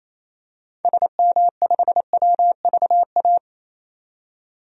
Morse code, heard as a keyed tone, about 28 wpm, 700 Hz. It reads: SM5WVA
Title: DXCC-100 Call Signs - 6 Characters - 28wpm